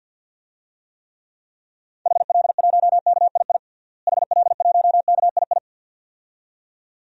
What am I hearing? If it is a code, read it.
Morse code, heard as a keyed tone, about 49 wpm, 700 Hz. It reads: HL1KII HL1KII
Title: DXCC-100 Call Signs - 6 Characters - Repeated 2x - 50wpm